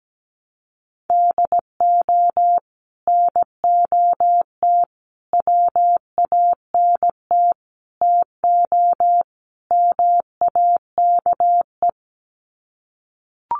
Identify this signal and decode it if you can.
Morse code, heard as a keyed tone, about 17 wpm, 700 Hz. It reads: DO NOT WANT TO MAKE
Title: Sets of 5 Words Encore 17wpm